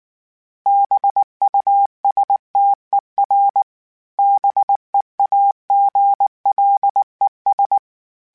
Morse code, heard as a keyed tone, about 19 wpm, 800 Hz. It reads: BUSTER BEAGLES
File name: Misc / Misc-mp3s / BIG @18